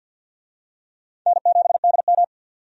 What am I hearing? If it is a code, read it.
Morse code, heard as a keyed tone, about 48 wpm, 700 Hz. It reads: N7DM
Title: USA Call Signs - 4 Characters - 50wpm